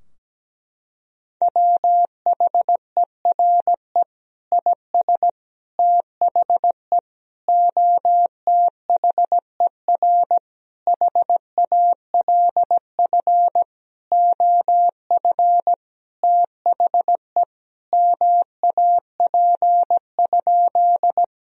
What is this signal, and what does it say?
Morse code, heard as a keyed tone, about 17 wpm, 700 Hz. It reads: WHERE IS THE OTHER HALF OF THE MAP?
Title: Sentences from Top 1000 Words - Review Part 2 - 17wpm